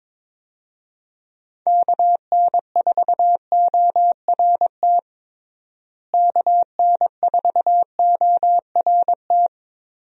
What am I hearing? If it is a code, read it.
Morse code, heard as a keyed tone, about 21 wpm, 700 Hz. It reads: KN4ORT KN4ORT
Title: USA Call Signs - 6 Characters - Repeated 2x - 22wpm